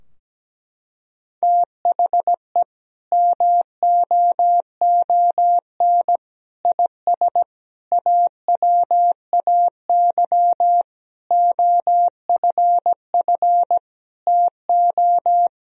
Morse code, heard as a keyed tone, about 17 wpm, 700 Hz. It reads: THE MOON IS AWAY OFF TO
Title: Sentences from Top 1000 Words - Review Part 3 - 17wpm